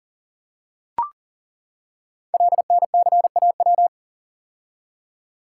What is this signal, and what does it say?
Morse code, heard as a keyed tone, about 37 wpm, 700 Hz. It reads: LNCAW
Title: Code Groups - 5 Letters - 40wpm